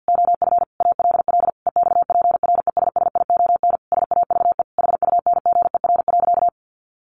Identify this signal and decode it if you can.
Morse code, heard as a keyed tone, about 50 wpm, 700 Hz. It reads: OF ALL EXPRESSION SAVE HUNGER.
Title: A Sound Of Thunder 50wpm